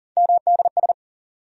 Morse code, heard as a keyed tone, about 40 wpm, 700 Hz. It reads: MDS
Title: USA Call Sign Postfixes - 3 Letters 40wpm